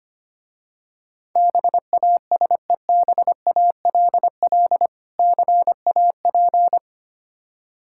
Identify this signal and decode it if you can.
Morse code, heard as a keyed tone, about 25 wpm, 700 Hz. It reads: BASEBALL CAP
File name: practice / 25wpm / Difficult Words with Hints - 25wpm